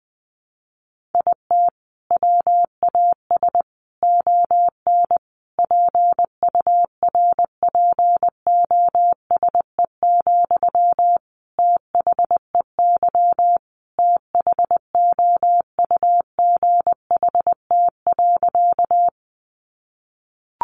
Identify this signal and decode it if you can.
Morse code, heard as a keyed tone, about 20 wpm, 700 Hz. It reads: IT WAS ON PURPOSE, THEY THOUGHT.
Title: Sentences from Top 800 Words 20wpm